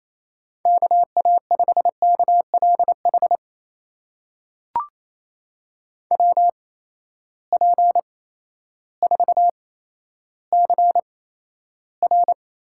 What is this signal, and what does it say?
Morse code, heard as a keyed tone, about 28 wpm, spaced as for 13 wpm, 700 Hz. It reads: KA5KLH WP4CR
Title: USA Call Signs - 6 Characters - 8x Character Spacing - 28wpm